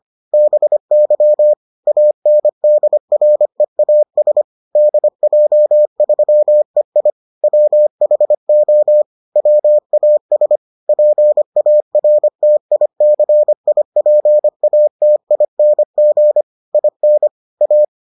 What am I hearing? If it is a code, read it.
Morse code, heard as a keyed tone, about 25 wpm, 600 Hz. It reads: BY ANDREAS DJ3EI WHO WAS PARTICIPATING IN A